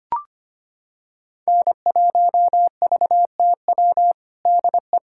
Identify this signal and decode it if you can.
Morse code, heard as a keyed tone, about 25 wpm, 700 Hz. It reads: N1VTW DE
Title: USA Callsign DE Callsign - 5 Characters - 25wpm